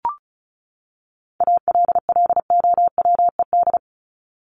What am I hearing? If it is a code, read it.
Morse code, heard as a keyed tone, about 34 wpm, 700 Hz. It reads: ALLOWED
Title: ED Suffix - Top 100 Words with Base Words 35wpm